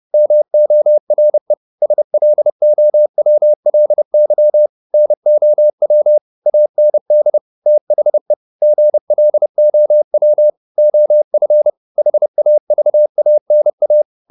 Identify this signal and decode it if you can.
Morse code, heard as a keyed tone, about 30 wpm, 600 Hz. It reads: MORE SLOWLY NOW AND THE GLOW OF HAVANA
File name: the old man and the sea / the old man and the sea 1 30wpm